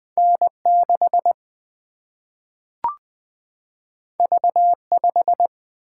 Morse code, heard as a keyed tone, about 20 wpm, 700 Hz. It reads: N6 V5
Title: Single-Letter Single-Number 20wpm